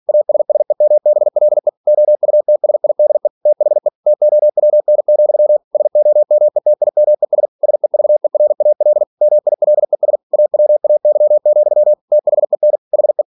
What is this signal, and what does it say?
Morse code, heard as a keyed tone, about 46 wpm, 600 Hz. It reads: ASSEMBLE OUTSIDE THE TOWN, SOMETIMES SEVERAL MILES AWAY, THEN HE
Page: Never 47wpm